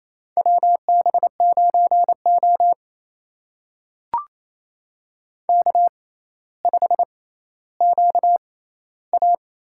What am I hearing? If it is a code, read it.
Morse code, heard as a keyed tone, about 28 wpm, spaced as for 14 wpm, 700 Hz. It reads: WB9O K5QA
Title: USA Call Signs - 4 Characters - 6x Character Spacing - 28wpm